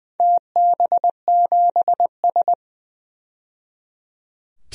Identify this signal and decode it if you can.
Morse code, heard as a keyed tone, about 20 wpm, 700 Hz. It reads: TB7S
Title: DXCC-100 Call Signs - 4 Characters - 20wpm